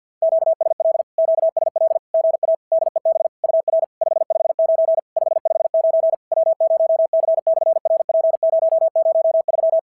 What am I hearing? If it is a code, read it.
Morse code, heard as a keyed tone, about 46 wpm, 660 Hz. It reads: QSL QSL GA DEB UR 559 559 W0C/RP003
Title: SOTA QSO Practice 50wpm